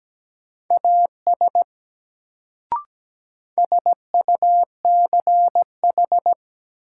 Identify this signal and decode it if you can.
Morse code, heard as a keyed tone, about 17 wpm, 700 Hz. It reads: AS SUCH
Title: Sets of 2 Words - New Word Drill - 17wpm